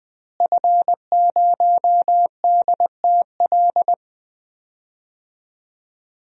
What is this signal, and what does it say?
Morse code, heard as a keyed tone, about 20 wpm, 700 Hz. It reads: F0DTL
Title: DXCC-100 Call Signs - 5 Characters - Repeated 3x - 20wpm